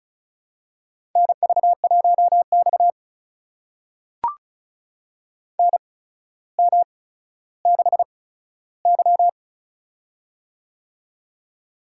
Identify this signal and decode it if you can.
Morse code, heard as a keyed tone, about 36 wpm, spaced as for 15 wpm, 700 Hz. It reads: NV1X NM6Y
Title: USA Call Signs - 4 Characters - 8x Character Spacing - 35wpm